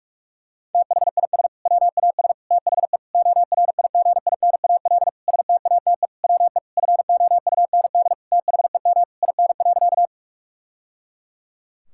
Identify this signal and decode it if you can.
Morse code, heard as a keyed tone, about 46 wpm, 700 Hz. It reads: THIS WAS THE ORIGINAL STATE WE FOUND THEM IN.
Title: Sentences from Top 1000 Words - Review Part 6 - 45wpm